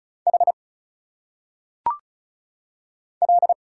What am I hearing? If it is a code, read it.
Morse code, heard as a keyed tone, about 34 wpm, 700 Hz. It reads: H L
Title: Single Letters - Phonetic Pronunciation 35wpm